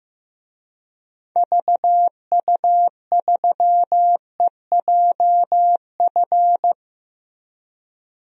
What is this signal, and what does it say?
Morse code, heard as a keyed tone, about 15 wpm, 700 Hz. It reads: VU3EJF
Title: DXCC-100 Call Signs - 6 Characters - Repeated 2x - 15wpm